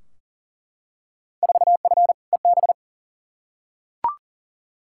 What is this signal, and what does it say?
Morse code, heard as a keyed tone, about 40 wpm, 700 Hz. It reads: 4F EB
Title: Field Day Exchanges - 40wpm